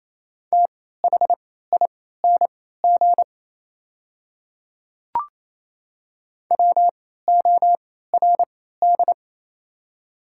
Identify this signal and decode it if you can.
Morse code, heard as a keyed tone, about 28 wpm, spaced as for 16 wpm, 700 Hz. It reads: THING WORD
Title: 3x Character Spacing - Top 100 Words - 28wpm